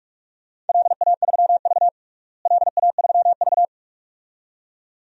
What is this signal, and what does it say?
Morse code, heard as a keyed tone, about 45 wpm, 700 Hz. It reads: LA3V LA3V
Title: DXCC-100 Call Signs - 4 Characters - Repeated 2x - 45wpm